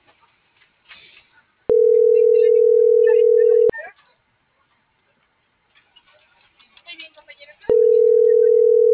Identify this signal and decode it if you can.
Ringback tone (North America), 440 + 480 Hz.